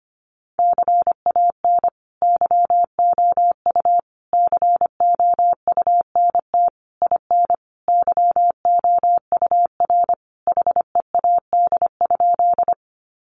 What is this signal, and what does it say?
Morse code, heard as a keyed tone, about 25 wpm, 700 Hz. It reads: CAN YOU COUNT IN YOUR HEAD?
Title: Sentences from Top 1500 Words 25wpm